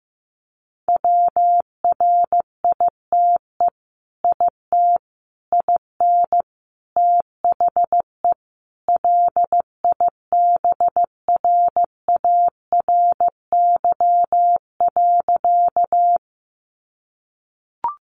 Morse code, heard as a keyed tone, about 15 wpm, 700 Hz. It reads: WRITE IT IN THE LIBRARY.
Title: Sentences from Top 600 Words 15wpm